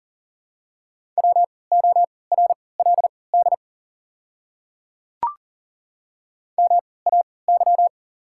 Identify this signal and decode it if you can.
Morse code, heard as a keyed tone, about 40 wpm, spaced as for 23 wpm, 690 Hz. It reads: WORLD MAY
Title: 3x Character Spacing - Top 300 Words - 40wpm